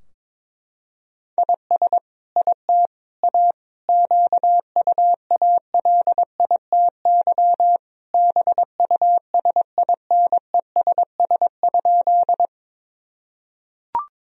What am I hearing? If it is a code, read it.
Morse code, heard as a keyed tone, about 22 wpm, 700 Hz. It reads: IS IT A QUALITY BUSINESS?
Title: Sentences from Top 1000 Words - Review Part 4 - 22wpm